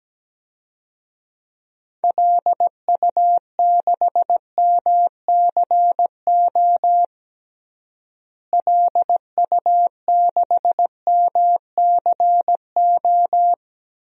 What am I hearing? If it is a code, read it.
Morse code, heard as a keyed tone, about 17 wpm, 700 Hz. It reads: LU6MCO LU6MCO
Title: DXCC-100 Call Signs - 6 Characters - Repeated 2x - 17wpm